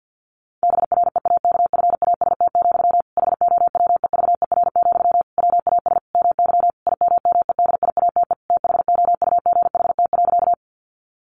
Morse code, heard as a keyed tone, about 49 wpm, 700 Hz. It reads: BREAKFAST, HOWEVER, WAS MY IMMEDIATE THOUGHT.
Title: Sentences from Top 1500 Words 50wpm